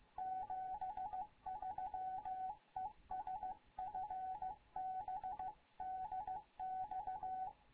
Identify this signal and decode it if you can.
Morse code, heard as a keyed tone, about 15 wpm, 750 Hz, with radio noise. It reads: 73 ES FB DX
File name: Morze-HA2MN / hangp / 2 11 GYAKORLAT